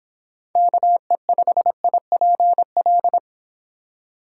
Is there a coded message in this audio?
Yes: Morse code from 0.6 to 3.2 s, heard as a keyed tone, about 25 wpm, 700 Hz. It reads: KE5IPL